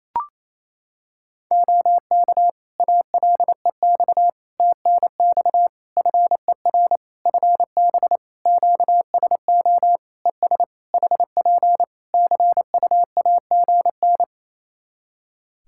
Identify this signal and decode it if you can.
Morse code, heard as a keyed tone, about 27 wpm, 700 Hz. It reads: OK ALEX TNX FER FB QSO ES HP CUAGN
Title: Practice QSO - Exchange 4 - Ending - 28wpm